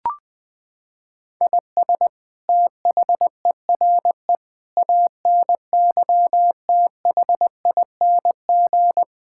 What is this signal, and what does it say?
Morse code, heard as a keyed tone, about 19 wpm, 700 Hz. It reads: IS THERE ANYTHING
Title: Sentences from Top 1000 Words - Review Part 6 - 20wpm